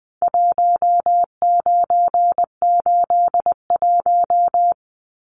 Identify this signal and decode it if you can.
Morse code, heard as a keyed tone, about 20 wpm, 700 Hz. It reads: 1981
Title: Numbers - Contest Style - 4-Digit 20wpm